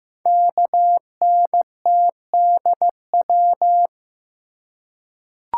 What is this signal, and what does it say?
Morse code, heard as a keyed tone, about 15 wpm, 700 Hz. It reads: KNTDW
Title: Code Groups - 5 Letters - 15wpm